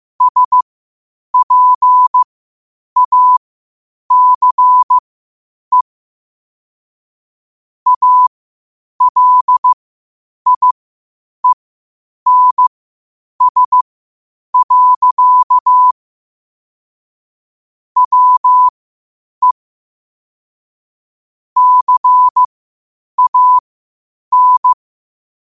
Morse code, heard as a keyed tone, about 15 wpm, spaced as for 8 wpm, 1000 Hz. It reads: SPACE ALIENS. WE CAN
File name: qotdfiles / 8wpm